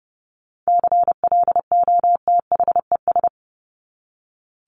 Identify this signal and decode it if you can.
Morse code, heard as a keyed tone, about 30 wpm, 700 Hz. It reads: CLOTHES
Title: Top 1751-2000 Words 30wpm